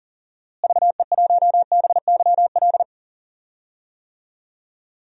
Morse code, heard as a keyed tone, about 37 wpm, 700 Hz. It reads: VE1BYL
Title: DXCC-100 Call Signs - 6 Characters - Repeated 3x - 40wpm